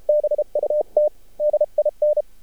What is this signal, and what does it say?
Morse code, heard as a keyed tone, about 32 wpm, 600 Hz. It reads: BUT DIN
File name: archived / CRSnet